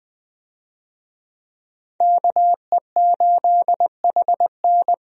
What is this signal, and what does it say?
Morse code, heard as a keyed tone, about 21 wpm, 700 Hz. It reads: KE8HN
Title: USA Call Signs - 5 Characters - 20wpm